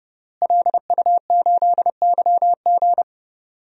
Morse code, heard as a keyed tone, about 30 wpm, 700 Hz. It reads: LU8YG
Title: DXCC-100 Call Signs - 5 Characters - Repeated 3x - 30wpm